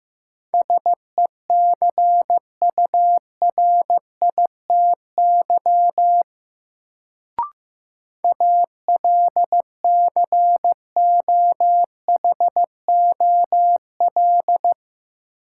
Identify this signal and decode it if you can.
Morse code, heard as a keyed tone, about 15 wpm, 700 Hz. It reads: SECURITY ALCOHOL
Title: Sentences from Top 1400 Words - New Word Drill Alt - 15wpm